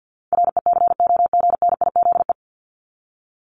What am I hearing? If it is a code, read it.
Morse code, heard as a keyed tone, about 50 wpm, 700 Hz. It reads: RECOGNIZE
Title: Top 1251-1500 Words 50wpm